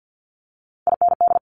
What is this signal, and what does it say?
Morse code, heard as a keyed tone, about 50 wpm, 700 Hz. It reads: IND
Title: Multi-letter Phonemes Quick Fire 50wpm